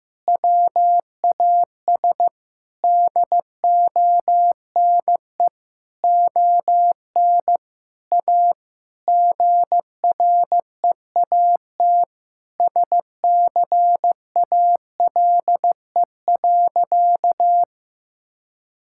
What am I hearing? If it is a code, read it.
Morse code, heard as a keyed tone, about 16 wpm, 700 Hz. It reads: WAS DONE ON A GREAT SCALE.
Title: Sentences from Top 1000 Words - Review Part 5 - 15wpm